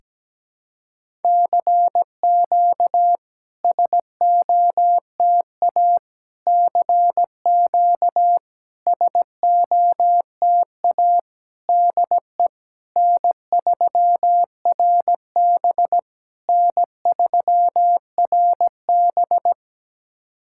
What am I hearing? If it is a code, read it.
Morse code, heard as a keyed tone, about 17 wpm, 700 Hz. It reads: CQ SOTA CQ SOTA DE N3RB N3RB
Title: SOTA QSO Practice 17wpm